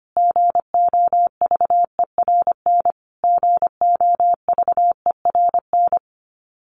Morse code, heard as a keyed tone, about 25 wpm, 700 Hz. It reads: GOVERN GOVERN
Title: Top 500 Words Repeated 25wpm